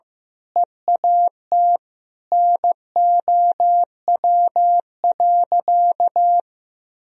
Morse code, heard as a keyed tone, about 15 wpm, 700 Hz. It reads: EAT NOW.
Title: Sentences from Top 300 Words and Abbreviations 15wpm